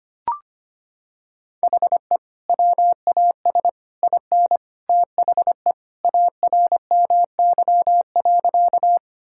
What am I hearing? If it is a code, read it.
Morse code, heard as a keyed tone, about 25 wpm, 700 Hz. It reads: HE WAS IN THE ARMY.
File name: practice / 25wpm / Sentences with 3-Word Phrases - 25wpm